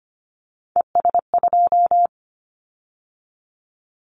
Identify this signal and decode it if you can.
Morse code, heard as a keyed tone, about 25 wpm, 700 Hz. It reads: ES2
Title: Callsign Prefixes for Top 50 DXCC - 25wpm